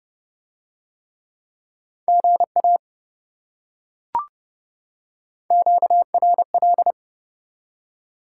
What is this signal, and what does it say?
Morse code, heard as a keyed tone, about 28 wpm, 700 Hz. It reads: GA QRL